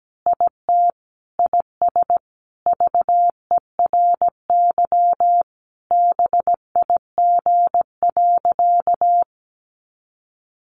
Morse code, heard as a keyed tone, about 17 wpm, 700 Hz. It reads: IT IS VERY BIG.
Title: Easy Phrases 17wpm